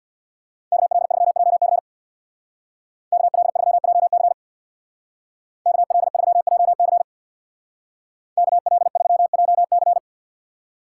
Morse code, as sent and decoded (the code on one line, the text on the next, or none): -.- .-.. ...-- .--- -.-. / -.- .-.. ...-- .--- -.-. / -.- .-.. ...-- .--- -.-. / -.- .-.. ...-- .--- -.-.
KL3JC KL3JC KL3JC KL3JC